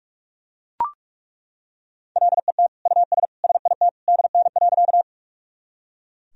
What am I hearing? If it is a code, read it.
Morse code, heard as a keyed tone, about 41 wpm, 690 Hz. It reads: LET US SIT DN.
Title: Sentences from Top 500 Words and Abbreviations 45wpm